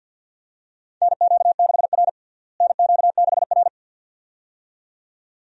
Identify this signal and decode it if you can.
Morse code, heard as a keyed tone, about 52 wpm, 700 Hz. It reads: NQ6R NQ6R
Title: USA Call Signs - 4 Characters - Repeated 2x - 50wpm